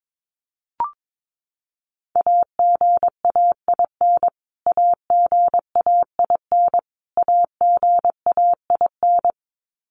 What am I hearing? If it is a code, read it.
Morse code, heard as a keyed tone, about 22 wpm, 700 Hz. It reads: AGAIN AGAIN AGAIN
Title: Top 300 Words Repeated 22wpm